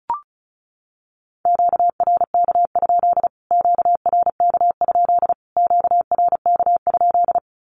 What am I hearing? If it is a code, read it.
Morse code, heard as a keyed tone, about 34 wpm, 700 Hz. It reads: QRK? QRK? QRK?
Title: Q-Codes Repeated 3x 35wpm